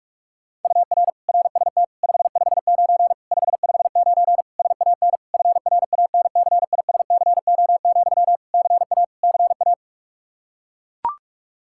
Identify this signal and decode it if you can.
Morse code, heard as a keyed tone, about 41 wpm, 700 Hz. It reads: UR RST 559 559 SAN FRANCISCO, CA CA
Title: Practice QSO - Exchange 1 - Beginning - 45wpm